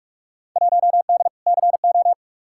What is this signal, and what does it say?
Morse code, heard as a keyed tone, about 42 wpm, 700 Hz. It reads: 1D CO
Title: Field Day Exchanges - 45wpm